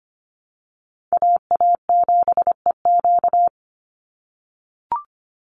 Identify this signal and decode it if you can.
Morse code, heard as a keyed tone, about 25 wpm, 700 Hz. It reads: AA7EQ
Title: NA SOTA Chaser Callsigns 25wpm